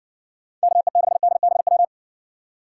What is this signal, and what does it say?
Morse code, heard as a keyed tone, about 60 wpm, 700 Hz. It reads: KE6DBW